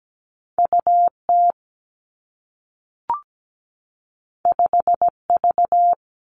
Morse code, heard as a keyed tone, about 17 wpm, 700 Hz. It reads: UT 5V
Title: Two Challenging Letter-Number Combinations 17wpm